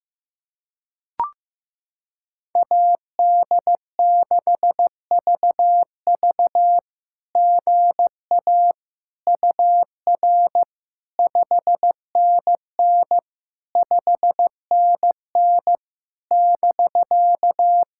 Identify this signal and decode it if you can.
Morse code, heard as a keyed tone, about 16 wpm, 700 Hz. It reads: AD6VV GA UR 5NN 5NN <BK>
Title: SOTA QSO Practice 15wpm